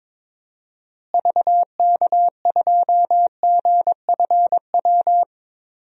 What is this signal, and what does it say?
Morse code, heard as a keyed tone, about 22 wpm, 700 Hz. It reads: VK2GFW
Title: DXCC-100 Call Signs - 6 Characters - Repeated 2x - 22wpm